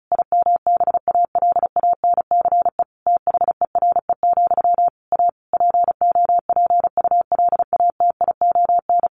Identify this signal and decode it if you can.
Morse code, heard as a keyed tone, about 34 wpm, 700 Hz. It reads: IMBALANCE THERE, A POPULATION
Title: A Sound Of Thunder 35wpm